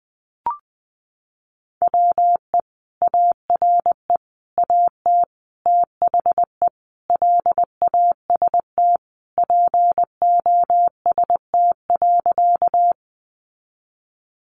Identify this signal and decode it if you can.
Morse code, heard as a keyed tone, about 20 wpm, 700 Hz. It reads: WE ARE AT THE LAST POST.
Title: Sentences from Top 1000 Words 20wpm